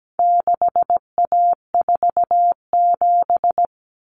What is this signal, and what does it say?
Morse code, heard as a keyed tone, about 17 wpm, 700 Hz. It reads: 6A47
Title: Common Cut Numbers - 4-Digit 17wpm